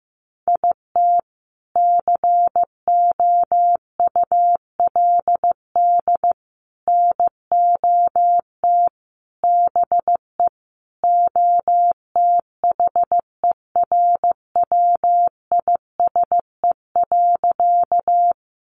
Morse code, heard as a keyed tone, about 15 wpm, 700 Hz. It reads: IT COULD NOT BE OTHERWISE.
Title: Sentences from Top 1200 Words 15wpm